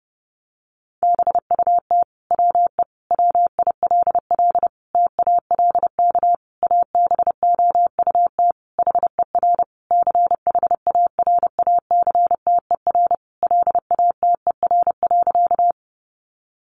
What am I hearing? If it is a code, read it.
Morse code, heard as a keyed tone, about 30 wpm, 700 Hz. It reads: BUT WE WILL TALK ABOUT HER CHARACTER LATER.
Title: Sentences from Top 1200 Words 30wpm